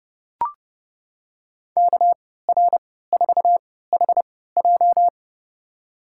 Morse code, heard as a keyed tone, about 31 wpm, spaced as for 17 wpm, 700 Hz. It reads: KR4HJ
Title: USA Call Signs - 5 Characters - 3x Character Spacing - 30wpm